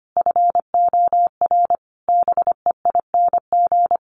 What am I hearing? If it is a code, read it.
Morse code, heard as a keyed tone, about 25 wpm, 700 Hz. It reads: FOR BEING
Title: Sets of 5 Words 25wpm